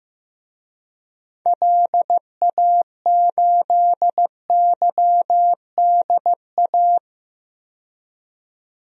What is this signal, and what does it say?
Morse code, heard as a keyed tone, about 15 wpm, 700 Hz. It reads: LA8YDA
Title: DXCC-100 Call Signs - 6 Characters - 15wpm